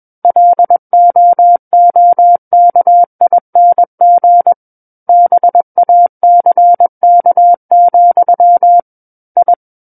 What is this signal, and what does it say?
Morse code, heard as a keyed tone, about 21 wpm, 700 Hz. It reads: LOOKING BACK, I
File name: Book One / 12-Chapter 12 - 21 WPM